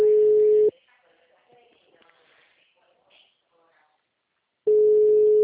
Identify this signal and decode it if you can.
Ringback tone (Europe, 425 Hz).